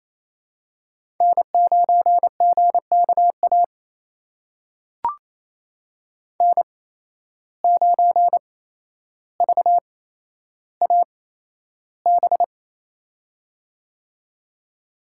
Morse code, heard as a keyed tone, about 28 wpm, spaced as for 12 wpm, 700 Hz. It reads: N9GKA N9VAB
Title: USA Call Signs - 5 Characters - 8x Character Spacing - 28wpm